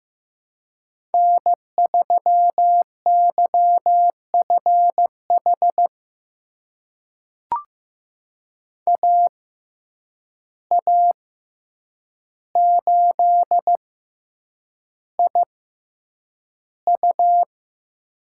Morse code, heard as a keyed tone, about 15 wpm, spaced as for 8 wpm, 700 Hz. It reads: N3YFH AA8IU
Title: USA Call Signs - 5 Characters - 6x Character Spacing - 15wpm